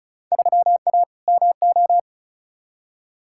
Morse code, heard as a keyed tone, about 36 wpm, 700 Hz. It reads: 3A MO
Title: Field Day Exchanges - 35wpm